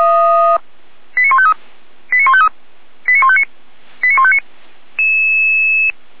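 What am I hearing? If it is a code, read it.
Five-tone selective call (ZVEI), heard as quick runs of five tones, tones of about 70 ms. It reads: 89153 89153 89159 89159
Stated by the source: ZVEI five-tone selective call 89150, received on SDR.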